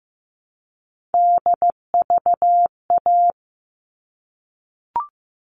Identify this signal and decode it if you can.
Morse code, heard as a keyed tone, about 15 wpm, 700 Hz. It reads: DVA
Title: All Cut Numbers - 3-Digit 15wpm